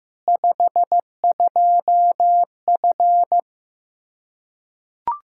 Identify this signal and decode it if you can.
Morse code, heard as a keyed tone, about 15 wpm, 700 Hz. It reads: 52F
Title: Practice QSO - Exchange 2 - Info - 15wpm